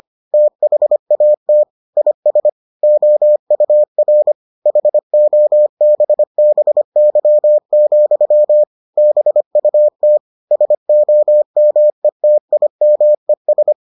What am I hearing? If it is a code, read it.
Morse code, heard as a keyed tone, about 25 wpm, 600 Hz. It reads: THAT IS OUR HOBBY, BUT SOMETIMES